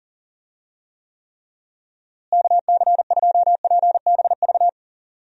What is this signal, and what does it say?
Morse code, heard as a keyed tone, about 40 wpm, 700 Hz. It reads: KC2PBV
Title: USA Call Signs - 6 Characters - Repeated 3x - 40wpm